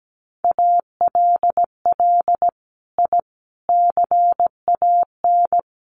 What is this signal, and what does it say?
Morse code, heard as a keyed tone, about 17 wpm, 700 Hz. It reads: ALL I CAN
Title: Sets of 5 Words Encore2 17wpm